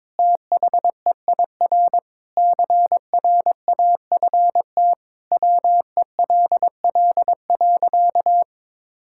Morse code, heard as a keyed tone, about 22 wpm, 700 Hz. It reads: THEIR CRAFT WELL.
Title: Sentences from Top 1000 Words - Review Part 6 - 22wpm